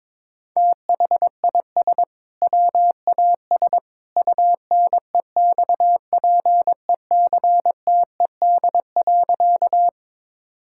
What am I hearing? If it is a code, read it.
Morse code, heard as a keyed tone, about 22 wpm, 700 Hz. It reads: THIS WAS UNEXPECTED.
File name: practice / 22wpm / Sentences with 2-Word Phrases - 22wpm